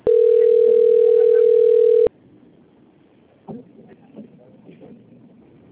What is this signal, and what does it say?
Ringback tone (North America), 440 + 480 Hz.